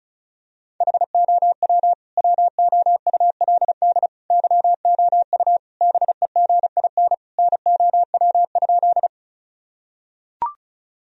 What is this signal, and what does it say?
Morse code, heard as a keyed tone, about 34 wpm, 700 Hz. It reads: HOW WOULD YOU BEGIN NOW?
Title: Sentences from Top 1000 Words - Review Part 1 - 35wpm